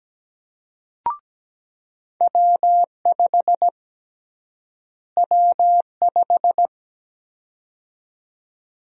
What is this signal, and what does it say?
Morse code, heard as a keyed tone, about 17 wpm, 700 Hz. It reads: W5 W5
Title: USA Call Sign Prefixes - Repeated 2x - 17wpm